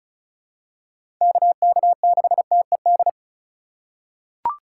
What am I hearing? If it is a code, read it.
Morse code, heard as a keyed tone, about 36 wpm, 700 Hz. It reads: KK6TED
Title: USA Call Signs - 6 Characters - 4x Character Spacing - 35wpm